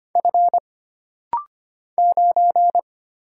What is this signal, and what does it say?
Morse code, heard as a keyed tone, about 25 wpm, 700 Hz. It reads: F 9
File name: Single Character (Letter:Number:Punctuation:Prosign) - Rapid-Fire - 25wpm